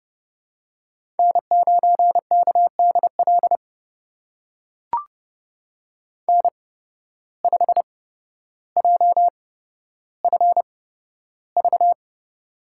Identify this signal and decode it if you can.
Morse code, heard as a keyed tone, about 30 wpm, spaced as for 13 wpm, 700 Hz. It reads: N9KDL N5JFV
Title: USA Call Signs - 5 Characters - 8x Character Spacing - 30wpm